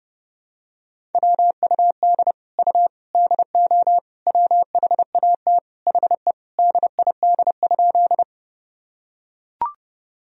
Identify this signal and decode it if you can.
Morse code, heard as a keyed tone, about 32 wpm, 700 Hz. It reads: WUD U DO WHAT HE DID?
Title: Sentences from Top 100 Words and Abbreviations 30wpm